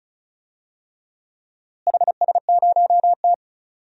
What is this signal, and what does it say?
Morse code, heard as a keyed tone, about 36 wpm, 700 Hz. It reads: HS0T
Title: DXCC-100 Call Signs - 4 Characters - Repeated 3x - 35wpm